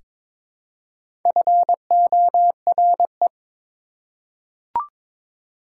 Morse code, heard as a keyed tone, about 22 wpm, 700 Hz. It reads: FORE
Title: Prefixes and Suffixes 22wpm